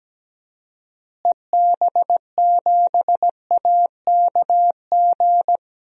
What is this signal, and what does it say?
Morse code, heard as a keyed tone, about 17 wpm, 700 Hz. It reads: EB7AKG
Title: DXCC-100 Call Signs - 6 Characters - Repeated 3x - 17wpm